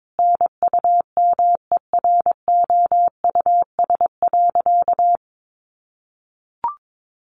Morse code, heard as a keyed tone, about 22 wpm, 700 Hz. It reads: NUMEROUS.
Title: Sentences from Top 1400 Words 22wpm